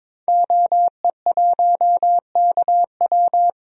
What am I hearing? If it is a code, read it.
Morse code, heard as a keyed tone, about 22 wpm, 700 Hz. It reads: OE1KW
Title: DXCC-100 Call Signs - 5 Characters - 22wpm